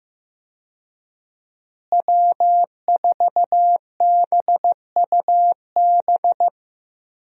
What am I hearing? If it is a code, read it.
Morse code, heard as a keyed tone, about 15 wpm, 700 Hz. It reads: W4BUB
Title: USA Call Signs - 5 Characters - Repeated 3x - 15wpm